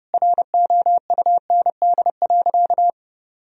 Morse code, heard as a keyed tone, about 30 wpm, 700 Hz. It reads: ROUND.
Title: Sentences from Top 1000 Words - Review Part 1 - 30wpm